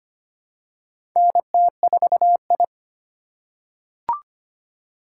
Morse code, heard as a keyed tone, about 25 wpm, 700 Hz. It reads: NT4I
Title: USA Call Signs - 4 Characters - 2x Character Spacing - 25wpm